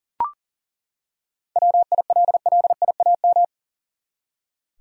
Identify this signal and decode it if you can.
Morse code, heard as a keyed tone, about 37 wpm, 700 Hz. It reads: WILLIAM
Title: USA Common Names - 40wpm